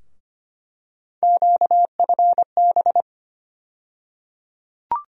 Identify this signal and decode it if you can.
Morse code, heard as a keyed tone, about 25 wpm, 700 Hz. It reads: QFB
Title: USA Call Sign Postfixes - 3 Letters 25wpm